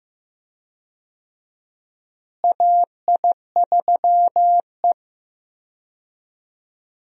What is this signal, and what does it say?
Morse code, heard as a keyed tone, about 16 wpm, 700 Hz. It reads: AI3E
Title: USA Call Signs - 4 Characters - Repeated 3x - 15wpm